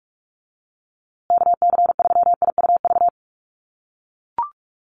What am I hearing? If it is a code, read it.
Morse code, heard as a keyed tone, about 44 wpm, 700 Hz. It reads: KC3IUV
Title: USA Call Signs Repeated 2x 45wpm